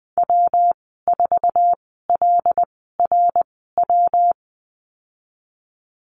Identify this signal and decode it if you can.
Morse code, heard as a keyed tone, about 20 wpm, spaced as for 15 wpm, 700 Hz. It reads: W4LRW
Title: 219 Lesson 35g - Number 0 - Call Signs 20wpm